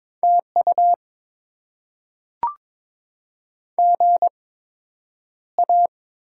Morse code, heard as a keyed tone, about 21 wpm, spaced as for 9 wpm, 700 Hz. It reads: TU GA
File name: Top 90 QSO Elements - 8x Character Spacing - 22wpm